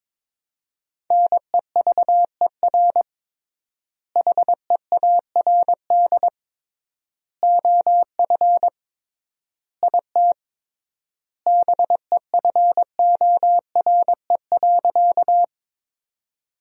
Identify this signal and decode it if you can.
Morse code, heard as a keyed tone, about 22 wpm, 700 Hz. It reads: NEVER HEARD OF IT BEFORE.
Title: 3x Word-Spacing - Sentences from Top 400 Words - 22wpm